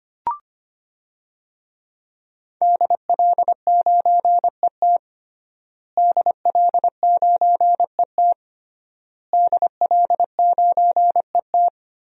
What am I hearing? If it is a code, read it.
Morse code, heard as a keyed tone, about 25 wpm, 700 Hz. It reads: DL9ET DL9ET DL9ET
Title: DXCC-100 Call Signs - 5 Characters - Repeated 3x - 25wpm